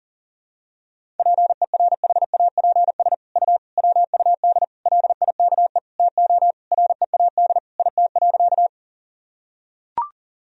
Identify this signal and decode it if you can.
Morse code, heard as a keyed tone, about 40 wpm, 700 Hz. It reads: PERHAPS U WUD LIKE TO READ IT.
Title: Sentences from Top 500 Words and Abbreviations 40wpm